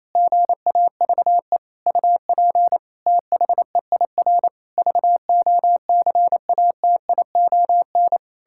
Morse code, heard as a keyed tone, about 28 wpm, 700 Hz. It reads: GAVE UP THEIR VOCATION
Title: Pangram Challenge v2 - 28wpm